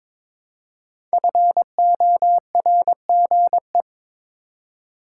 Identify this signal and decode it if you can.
Morse code, heard as a keyed tone, about 22 wpm, 700 Hz. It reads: FORGE
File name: 5 Letter Words - First 3 Letters Given - 22wpm